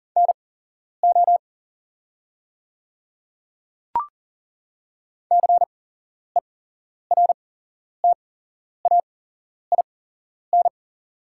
Morse code, heard as a keyed tone, about 40 wpm, spaced as for 11 wpm, 690 Hz. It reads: NO CERTAIN
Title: 8x Character Spacing - Top 500 Words - 40wpm